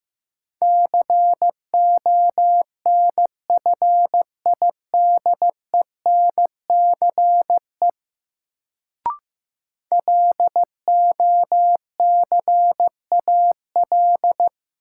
Morse code, heard as a keyed tone, about 16 wpm, 700 Hz. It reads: CONFIDENCE LOCAL
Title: Sentences from Top 1500 Words - New Word Drill Alt - 15wpm